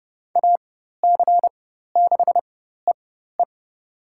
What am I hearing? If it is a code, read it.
Morse code, heard as a keyed tone, about 30 wpm, spaced as for 14 wpm, 700 Hz. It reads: AC6EE
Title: USA Call Signs - 5 Characters - 4x Character Spacing - 30wpm